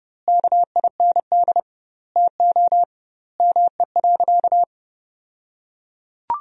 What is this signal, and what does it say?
Morse code, heard as a keyed tone, about 31 wpm, 700 Hz. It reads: KIND TO ME.
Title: 2x Word-Spacing - Sentences from Top 200 Words - 30wpm